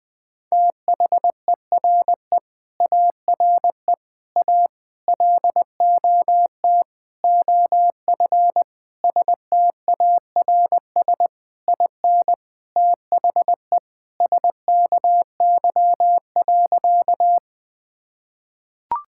Morse code, heard as a keyed tone, about 21 wpm, 700 Hz. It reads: THERE ARE A LOT OF STARS IN THE SKY.
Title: Sentences with 4-Word Phrases - 20wpm